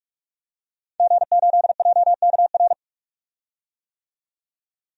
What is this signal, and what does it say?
Morse code, heard as a keyed tone, about 46 wpm, 700 Hz. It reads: G8JKR
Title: DXCC-100 Call Signs - 5 Characters - 45wpm